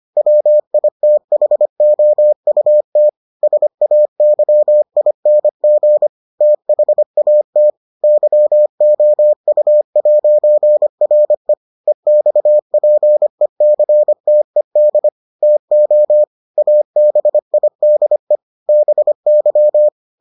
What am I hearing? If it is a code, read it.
Morse code, heard as a keyed tone, about 25 wpm, 600 Hz. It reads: WITHOUT SAYING THAT YOU'RE EXPECTED TO ABIDE BY